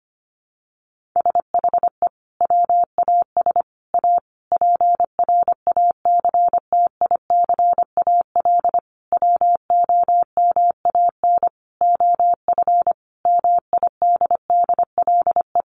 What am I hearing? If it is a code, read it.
Morse code, heard as a keyed tone, about 25 wpm, 700 Hz. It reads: SHE WAS A PRACTICAL WOMAN OF MIDDLE
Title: Sentences from Top 1400 Words 25wpm